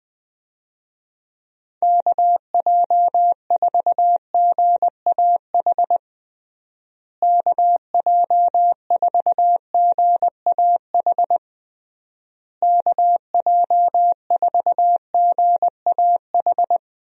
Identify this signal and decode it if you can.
Morse code, heard as a keyed tone, about 20 wpm, 700 Hz. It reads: KJ4GAH KJ4GAH KJ4GAH
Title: USA Call Signs - 6 Characters - Repeated 3x - 20wpm